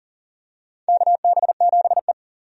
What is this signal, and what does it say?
Morse code, heard as a keyed tone, about 40 wpm, 700 Hz. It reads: KB7E
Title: USA Call Signs - 4 Characters - 40wpm